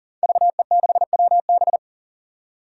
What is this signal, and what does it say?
Morse code, heard as a keyed tone, about 39 wpm, 700 Hz. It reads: VE6WB
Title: DXCC-100 Call Signs - 5 Characters - 40wpm